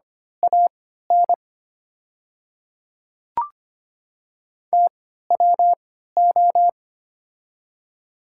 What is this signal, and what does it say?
Morse code, heard as a keyed tone, about 25 wpm, spaced as for 14 wpm, 690 Hz. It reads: AN TWO
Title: 3x Character Spacing - Top 500 Words - 25wpm